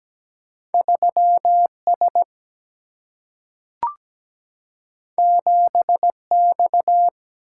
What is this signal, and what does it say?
Morse code, heard as a keyed tone, about 17 wpm, 700 Hz. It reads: 3 S 7 X
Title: Single-Letter Single-Number 17wpm